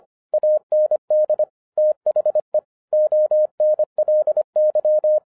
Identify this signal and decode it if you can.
Morse code, heard as a keyed tone, about 24 wpm, 610 Hz. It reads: AND THE ONLY
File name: FIRST FISTS CONTACT-25wpm